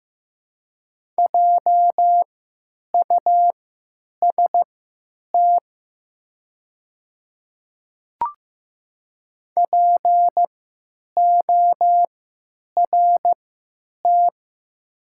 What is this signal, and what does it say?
Morse code, heard as a keyed tone, about 15 wpm, spaced as for 9 wpm, 700 Hz. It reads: JUST PORT
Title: 3x Character Spacing - Top 200 Words - 15wpm